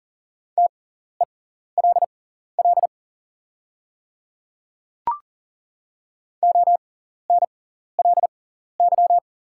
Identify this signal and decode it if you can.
Morse code, heard as a keyed tone, about 40 wpm, spaced as for 14 wpm, 700 Hz. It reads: TELL ONLY
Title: 6x Character Spacing - Top 200 Words - 40wpm